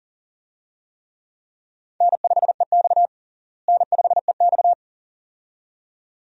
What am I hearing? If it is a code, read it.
Morse code, heard as a keyed tone, about 40 wpm, 700 Hz. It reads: N5EX N5EX
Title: USA Call Signs - 4 Characters - Repeated 2x - 40wpm